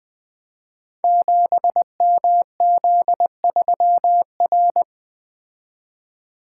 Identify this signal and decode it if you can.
Morse code, heard as a keyed tone, about 20 wpm, 700 Hz. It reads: 7MZ3R
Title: Code Groups - 5 Letter:Numbers - 20wpm